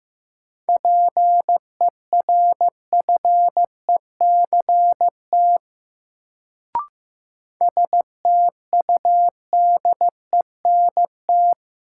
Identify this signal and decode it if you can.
Morse code, heard as a keyed tone, about 15 wpm, 700 Hz. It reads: PERFECT STUDENT
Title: Sentences from Top 1000 Words - New Word Drill - 15wpm